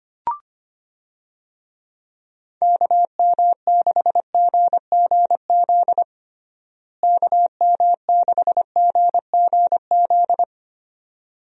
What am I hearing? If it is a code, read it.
Morse code, heard as a keyed tone, about 25 wpm, 700 Hz. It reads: KM6GGZ KM6GGZ
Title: USA Call Signs - 6 Characters - Repeated 2x - 25wpm